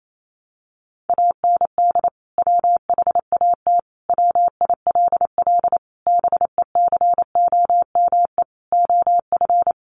Morse code, heard as a keyed tone, about 28 wpm, 700 Hz. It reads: AND WHAT WILL BECOME OF
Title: Sentences from Top 600 Words 28wpm